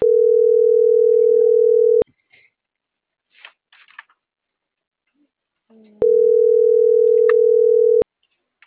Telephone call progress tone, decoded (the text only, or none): ringback tone